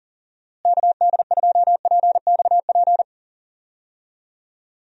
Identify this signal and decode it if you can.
Morse code, heard as a keyed tone, about 37 wpm, 700 Hz. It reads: KD2PXP
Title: USA Call Signs - 6 Characters - Repeated 3x - 40wpm